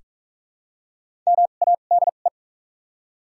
Morse code, heard as a keyed tone, about 43 wpm, spaced as for 32 wpm, 700 Hz. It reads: MADE
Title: 2x Character Spacing - Top 300 Words - 45wpm